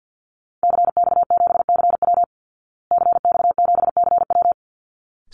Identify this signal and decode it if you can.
Morse code, heard as a keyed tone, about 50 wpm, 700 Hz. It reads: CX7CW CX7CW
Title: Call Signs Repeated 3x - 50wpm